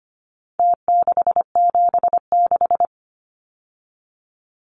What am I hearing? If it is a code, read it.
Morse code, heard as a keyed tone, about 25 wpm, 700 Hz. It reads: T676
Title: Numbers - Contest Style - 4-Digit 25wpm